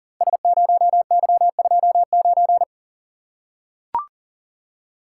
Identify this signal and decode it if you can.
Morse code, heard as a keyed tone, about 40 wpm, 700 Hz. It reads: S0Y29
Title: Code Groups - 5 Letter:Numbers - 40wpm